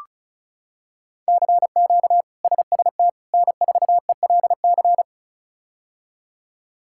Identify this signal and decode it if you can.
Morse code, heard as a keyed tone, about 34 wpm, 700 Hz. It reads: CQ SST N4ELC
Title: K1USN SST QSO Practice - 35wpm